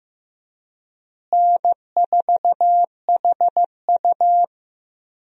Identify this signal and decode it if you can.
Morse code, heard as a keyed tone, about 15 wpm, 700 Hz. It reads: N4HU
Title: USA Call Signs - 4 Characters - Repeated 3x - 15wpm